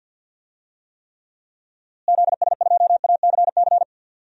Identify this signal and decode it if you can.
Morse code, heard as a keyed tone, about 46 wpm, 700 Hz. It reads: ZS2ACC
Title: DXCC-100 Call Signs - 6 Characters - Repeated 3x - 50wpm